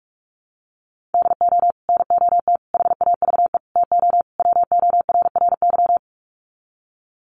Morse code, heard as a keyed tone, about 44 wpm, 700 Hz. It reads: DO NOT HAVE TO WORRY
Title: Sets of 5 Words 45wpm